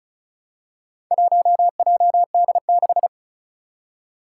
Morse code, heard as a keyed tone, about 36 wpm, 700 Hz. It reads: 1JD6
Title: Four Challenging Letter-Number Combinations 35wpm